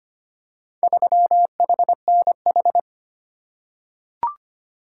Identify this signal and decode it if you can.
Morse code, heard as a keyed tone, about 25 wpm, 700 Hz. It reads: 3HNH
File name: practice / 25wpm / Four Letter-Number Combinations - 25wpm